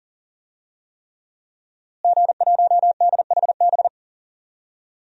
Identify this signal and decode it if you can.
Morse code, heard as a keyed tone, about 38 wpm, 700 Hz. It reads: G1DHB
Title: DXCC-100 Call Signs - 5 Characters - Repeated 3x - 40wpm